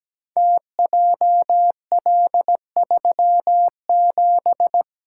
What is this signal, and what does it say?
Morse code, heard as a keyed tone, about 17 wpm, 700 Hz. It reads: TJL37
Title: Code Groups - 5 Letter:Numbers - 17wpm